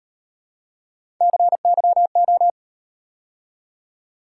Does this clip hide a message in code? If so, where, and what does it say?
Yes: Morse code from 1.2 to 2.5 s, heard as a keyed tone, about 37 wpm, 700 Hz. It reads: CYO